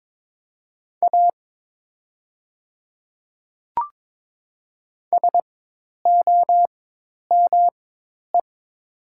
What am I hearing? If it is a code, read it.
Morse code, heard as a keyed tone, about 22 wpm, spaced as for 10 wpm, 700 Hz. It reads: A SOME
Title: 4x Character Spacing - Top 100 Words - 22wpm